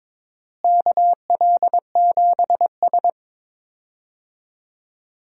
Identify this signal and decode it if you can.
Morse code, heard as a keyed tone, about 21 wpm, 700 Hz. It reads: KL7S
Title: DXCC-100 Call Signs - 4 Characters - Repeated 3x - 22wpm